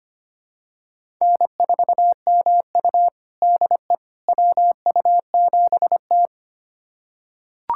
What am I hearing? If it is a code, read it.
Morse code, heard as a keyed tone, about 25 wpm, 700 Hz. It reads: N4MU DE WU7T
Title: USA Callsign DE Callsign - 4 Characters - 25wpm